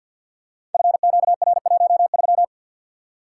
Morse code, heard as a keyed tone, about 50 wpm, 700 Hz. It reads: FQR13
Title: Code Groups - 5 Letter:Numbers - 50wpm